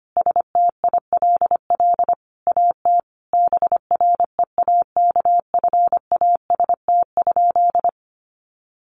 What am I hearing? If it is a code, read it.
Morse code, heard as a keyed tone, about 25 wpm, 700 Hz. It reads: STILL AT BREAKFAST?
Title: Sentences from Top 1500 Words 25wpm